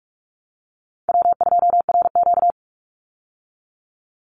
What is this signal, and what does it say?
Morse code, heard as a keyed tone, about 46 wpm, 700 Hz. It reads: W2RQ
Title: Easy Call Signs - 45wpm